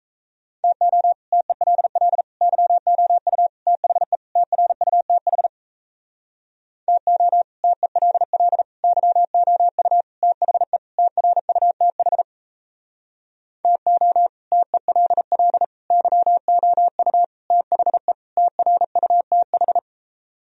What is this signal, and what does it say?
Morse code, heard as a keyed tone, about 37 wpm, 700 Hz. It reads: TO TELL YOU THE TRUTH TO TELL YOU THE TRUTH TO TELL YOU THE TRUTH